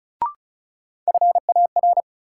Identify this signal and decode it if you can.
Morse code, heard as a keyed tone, about 36 wpm, 700 Hz. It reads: FAR
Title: Sets of 3 Words - New Word Drill - 35wpm